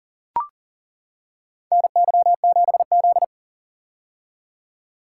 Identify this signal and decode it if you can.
Morse code, heard as a keyed tone, about 40 wpm, 700 Hz. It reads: NY7Z
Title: K1USN SST QSO Practice - 40wpm